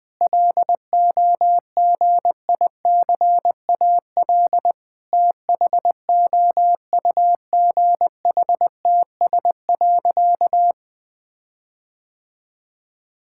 Morse code, heard as a keyed tone, about 20 wpm, 700 Hz. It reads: LOGICAL THOUGHTS.
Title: All-Age Appropriate Jokes - 20wpm